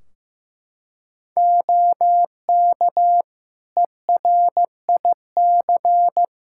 Morse code, heard as a keyed tone, about 15 wpm, 700 Hz. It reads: OK ERIC
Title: Practice QSO - Exchange 3 - More Info - 15wpm